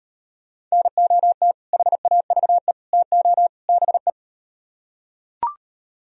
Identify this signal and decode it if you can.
Morse code, heard as a keyed tone, about 38 wpm, 700 Hz. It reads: NOT HAVE TO BE